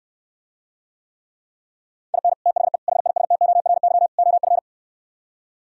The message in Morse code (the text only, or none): IN THE HISTORY OF